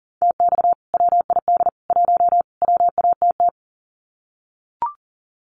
Morse code, heard as a keyed tone, about 40 wpm, 700 Hz. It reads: TX WID 1 WATT
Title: Common QSO Phrases 40wpm